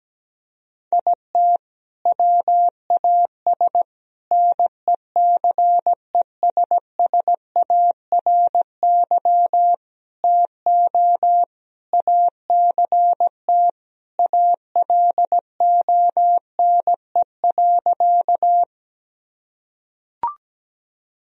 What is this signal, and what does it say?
Morse code, heard as a keyed tone, about 17 wpm, 700 Hz. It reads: IT WAS NECESSARY TO ACT ALONE.
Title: Sentences from Top 1000 Words - Review Part 1 - 17wpm